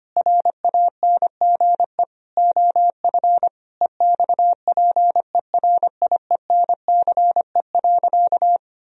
Morse code, heard as a keyed tone, about 25 wpm, 700 Hz. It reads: RANGE OF EXPERIENCE.
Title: Sentences from Top 1000 Words - Review Part 4 - 25wpm